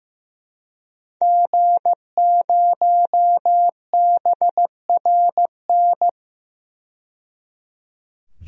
Morse code, heard as a keyed tone, about 16 wpm, 700 Hz. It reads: G0BRN
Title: DXCC-100 Call Signs - 5 Characters - 15wpm